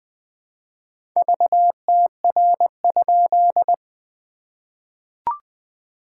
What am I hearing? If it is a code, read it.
Morse code, heard as a keyed tone, about 20 wpm, 700 Hz. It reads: VTR?
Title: SOTA QSO Practice 20wpm